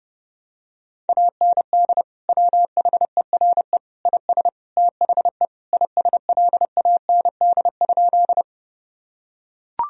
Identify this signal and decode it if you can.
Morse code, heard as a keyed tone, about 30 wpm, 700 Hz. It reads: AND WHERE IS THE ISLAND?
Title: Sentences from Top 1000 Words - Review Part 3 - 30wpm